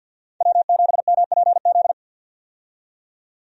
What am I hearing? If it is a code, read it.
Morse code, heard as a keyed tone, about 50 wpm, 700 Hz. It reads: W7MPZ
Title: USA Call Signs - 5 Characters - 50wpm